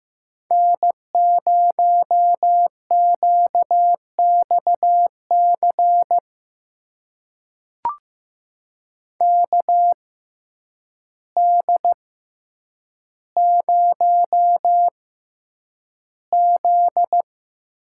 Morse code, heard as a keyed tone, about 15 wpm, spaced as for 8 wpm, 700 Hz. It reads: N0QXC KD0Z
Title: USA Call Signs - 6x Character Spacing - 15wpm